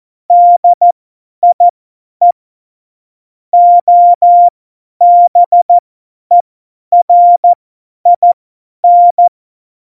Morse code, heard as a keyed tone, about 14 wpm, spaced as for 10 wpm, 700 Hz. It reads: DIE OBERIN
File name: Morsekurs / Witze / Witz 12. Januar 2021, 50 F70